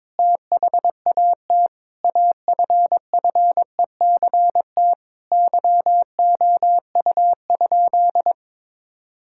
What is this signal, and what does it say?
Morse code, heard as a keyed tone, about 21 wpm, 700 Hz. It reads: THAT AFFECT YOU?
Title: Sentences from Top 1000 Words - Review Part 6 - 22wpm